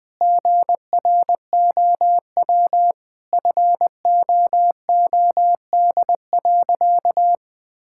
Morse code, heard as a keyed tone, about 20 wpm, 700 Hz. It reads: GROW FOOD.
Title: Sentences from Top 1000 Words - Review Part 6 - 20wpm